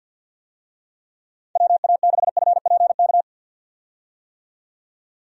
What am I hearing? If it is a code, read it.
Morse code, heard as a keyed tone, about 52 wpm, 700 Hz. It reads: WA6FPK
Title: USA Call Signs - 6 Characters - 50wpm